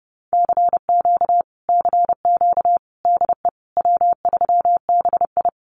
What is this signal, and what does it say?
Morse code, heard as a keyed tone, about 30 wpm, 700 Hz. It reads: CQ CQ DE W3BI
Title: Common QSO Phrases Encore 30wpm